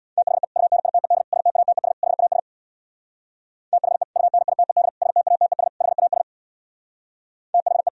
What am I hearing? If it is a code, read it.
Morse code, heard as a keyed tone, about 61 wpm, 700 Hz. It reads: THE UNITED STATES HAS THE UNITED STATES HAS THE
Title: Speed-Racing - Sets of 4 Words 50wpm